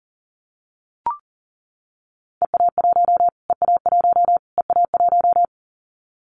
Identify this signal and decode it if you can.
Morse code, heard as a keyed tone, about 37 wpm, 700 Hz. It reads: EA1 EA1 EA1
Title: Callsign Prefixes for Top 25 DXCC - Repeated 3x - 40wpm